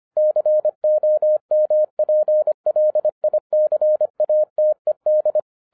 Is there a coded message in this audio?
Yes: Morse code from 0.2 to 5.4 s, heard as a keyed tone, about 24 wpm, 610 Hz. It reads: COMPLICATED